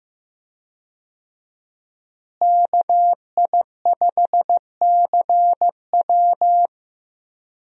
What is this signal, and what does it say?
Morse code, heard as a keyed tone, about 15 wpm, 700 Hz. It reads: KI5CW
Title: USA Call Signs - 5 Characters - Repeated 2x - 15wpm